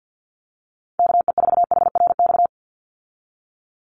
Morse code, heard as a keyed tone, about 50 wpm, 700 Hz. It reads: KE4HRX
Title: USA Call Signs 50wpm